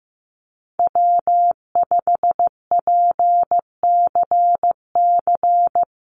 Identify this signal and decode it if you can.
Morse code, heard as a keyed tone, about 15 wpm, 700 Hz. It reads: W5PCC
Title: USA Call Signs 15wpm